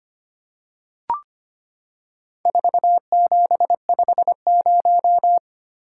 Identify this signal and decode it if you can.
Morse code, heard as a keyed tone, about 25 wpm, 700 Hz. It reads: 4750
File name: practice / 25wpm / Four Letter-Number Combinations - 25wpm